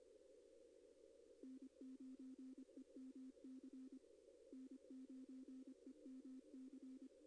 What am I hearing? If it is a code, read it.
Morse code, heard as a keyed tone, about 25 wpm, 260 Hz. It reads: N9EMC N9EMC
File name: contests / arrl-160-k5tr-op-w5tn -2200